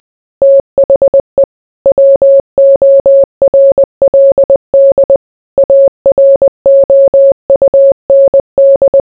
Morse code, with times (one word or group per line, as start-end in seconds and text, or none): 0.4-1.4 THE
1.9-5.2 WORLD
5.6-9.0 AROUND